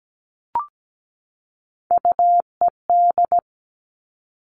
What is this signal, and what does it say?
Morse code, heard as a keyed tone, about 17 wpm, 700 Hz. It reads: UED
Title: All Cut Numbers - 3-Digit 17wpm